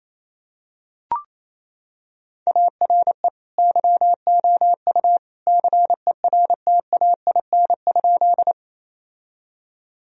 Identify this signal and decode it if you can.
Morse code, heard as a keyed tone, about 28 wpm, 700 Hz. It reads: ARE YOU CERTAIN?
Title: Sentences from Top 1000 Words - Review Part 2 - 28wpm